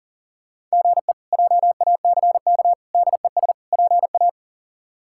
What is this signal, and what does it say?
Morse code, heard as a keyed tone, about 40 wpm, 700 Hz. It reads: GE JACK DES PA
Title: K1USN SST QSO Practice - 40wpm